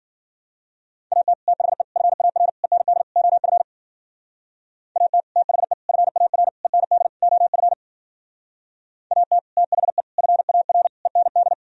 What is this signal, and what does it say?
Morse code, heard as a keyed tone, about 50 wpm, 700 Hz. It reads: AT THE FAR END OF AT THE FAR END OF AT THE FAR END
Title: Speed-Racing - Sets of 5 Words 40wpm